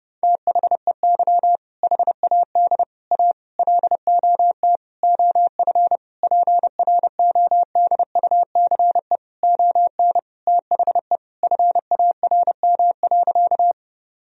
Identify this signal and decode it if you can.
Morse code, heard as a keyed tone, about 30 wpm, 700 Hz. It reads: THEY HAD A LOT OF PRODUCE ON THE FARM.
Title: Sentences from Top 1000 Words - Review Part 3 - 30wpm